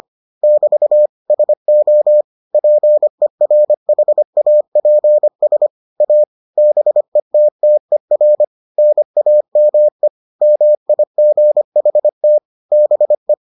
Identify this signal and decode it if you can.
Morse code, heard as a keyed tone, about 25 wpm, 600 Hz. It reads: <BT> SO PERHAPS A BETTER NAME MIGHT BE